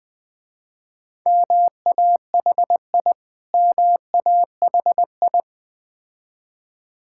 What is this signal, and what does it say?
Morse code, heard as a keyed tone, about 20 wpm, 700 Hz. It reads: MAHI MAHI
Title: Difficult Words with Hints - 20wpm